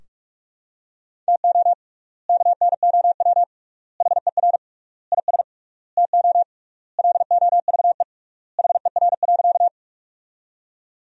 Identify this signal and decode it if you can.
Morse code, heard as a keyed tone, about 46 wpm, 700 Hz. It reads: TO KNOW HER IS TO LOVE HER.
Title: 3x Word-Spacing - Sentences from Top 400 Words - 45wpm